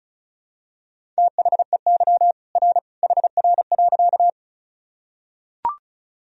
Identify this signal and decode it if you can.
Morse code, heard as a keyed tone, about 36 wpm, 700 Hz. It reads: THEY R HR.
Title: Sentences from Top 200 Words and Abbreviations 35wpm